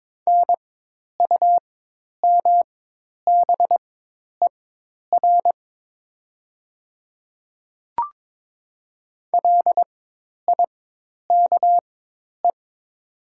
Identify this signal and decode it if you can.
Morse code, heard as a keyed tone, about 22 wpm, spaced as for 10 wpm, 700 Hz. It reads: NUMBER LIKE
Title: 4x Character Spacing - Top 200 Words - 22wpm